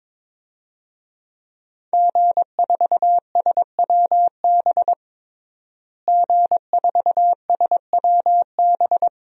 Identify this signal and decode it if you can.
Morse code, heard as a keyed tone, about 22 wpm, 700 Hz. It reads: G4SWB G4SWB
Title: DXCC-100 Call Signs - 5 Characters - Repeated 2x - 22wpm